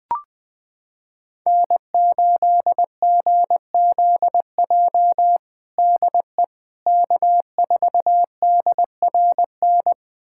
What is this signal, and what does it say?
Morse code, heard as a keyed tone, about 21 wpm, 700 Hz. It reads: N8GZJ DE K4DRN
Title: USA Callsign DE Callsign 20wpm